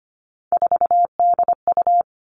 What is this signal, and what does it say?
Morse code, heard as a keyed tone, about 25 wpm, 700 Hz. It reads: 4DU
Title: Three Challenging Letter-Number Combinations 25wpm